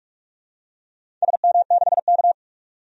Morse code, heard as a keyed tone, about 46 wpm, 700 Hz. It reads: SM6K
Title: DXCC-100 Call Signs - 4 Characters - 45wpm